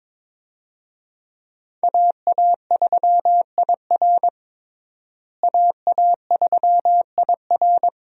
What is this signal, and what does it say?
Morse code, heard as a keyed tone, about 22 wpm, 700 Hz. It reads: AA3IR AA3IR
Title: USA Call Signs - 5 Characters - Repeated 3x - 22wpm